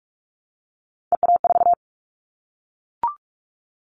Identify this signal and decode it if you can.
Morse code, heard as a keyed tone, about 44 wpm, 700 Hz. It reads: EA4
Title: Callsign Prefixes for Top 25 DXCC - Repeated 3x - 45wpm